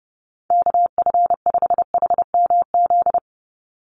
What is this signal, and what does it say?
Morse code, heard as a keyed tone, about 30 wpm, 700 Hz. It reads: KF5HMZ
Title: USA Call Signs 30wpm